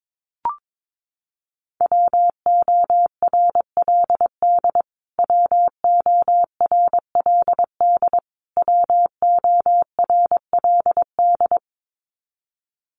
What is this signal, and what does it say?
Morse code, heard as a keyed tone, about 22 wpm, 700 Hz. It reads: WORLD WORLD WORLD
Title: Top 200 Words Repeated 22wpm